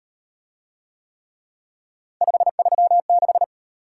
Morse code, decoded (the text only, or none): 536